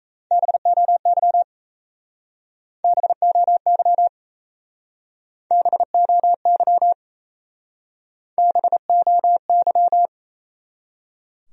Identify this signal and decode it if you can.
Morse code, heard as a keyed tone, about 37 wpm, 700 Hz. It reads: BOY BOY BOY BOY